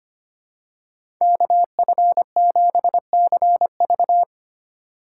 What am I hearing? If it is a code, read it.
Morse code, heard as a keyed tone, about 25 wpm, 700 Hz. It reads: KF7CV
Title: USA Call Signs - 2x Character Spacing - 25wpm